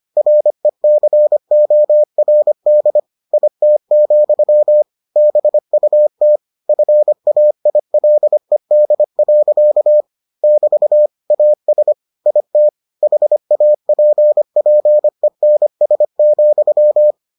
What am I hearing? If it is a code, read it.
Morse code, heard as a keyed tone, about 25 wpm, 600 Hz. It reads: RECORD IT, BUT FAILED. <BT> AS IT HAPPENS,